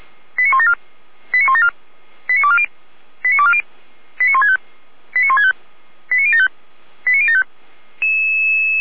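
Five-tone selective call (ZVEI), heard as quick runs of five tones, tones of about 70 ms. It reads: 89174 89174 89240 89240 89165 89165 89075 89075
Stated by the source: ZVEI five-tone selective call received by SDR, from the feuerwehr-orlamuende.org audio archive.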